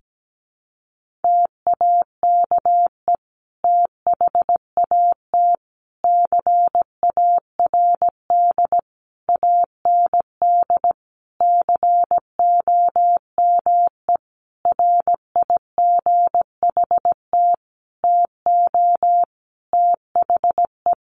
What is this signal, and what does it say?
Morse code, heard as a keyed tone, about 17 wpm, 700 Hz. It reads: TAKE THAT CARD AND COME RIGHT TO THE
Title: Sentences from Top 1400 Words 17wpm